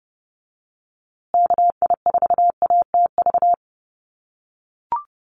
Morse code, heard as a keyed tone, about 30 wpm, 700 Hz. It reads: KI4ATV
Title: USA Call Signs Repeated 3x 30wpm